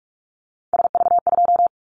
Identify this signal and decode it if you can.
Morse code, heard as a keyed tone, about 44 wpm, 700 Hz. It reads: SV2
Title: Callsign Prefixes for Top 50 DXCC - 45wpm